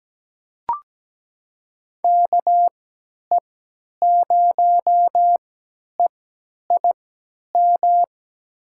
Morse code, heard as a keyed tone, about 17 wpm, spaced as for 10 wpm, 700 Hz. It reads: KE0EIM
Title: USA Call Signs - 6 Characters - 3x Character Spacing - 17wpm